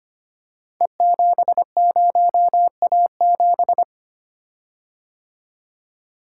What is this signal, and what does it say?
Morse code, heard as a keyed tone, about 25 wpm, 700 Hz. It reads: E70A7
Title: Code Groups - 5 Letter:Numbers - 25wpm